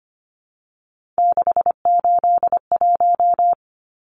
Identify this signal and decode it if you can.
Morse code, heard as a keyed tone, about 25 wpm, 700 Hz. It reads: 681
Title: Numbers - 3-Digit 25wpm